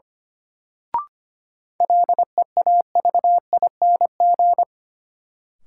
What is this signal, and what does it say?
Morse code, heard as a keyed tone, about 25 wpm, 700 Hz. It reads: LEAVING
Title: Sentences from Top 1000 Words - Review Part 4 - New Word Drill - 25wpm